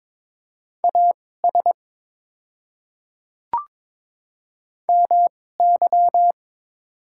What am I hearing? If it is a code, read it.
Morse code, heard as a keyed tone, about 22 wpm, spaced as for 16 wpm, 700 Hz. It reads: AS MY
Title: 2x Character Spacing - Top 200 Words - 22wpm